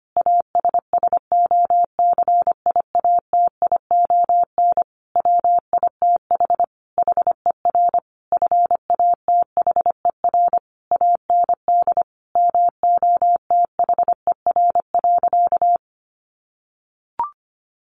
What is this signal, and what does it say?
Morse code, heard as a keyed tone, about 25 wpm, 700 Hz. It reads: ASSOCIATION WITH HER FATHER AND MOTHER.
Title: Sentences from Top 1100 Words 25wpm